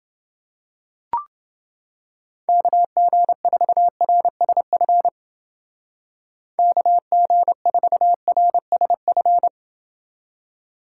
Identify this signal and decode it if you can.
Morse code, heard as a keyed tone, about 30 wpm, 700 Hz. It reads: KG4RSF KG4RSF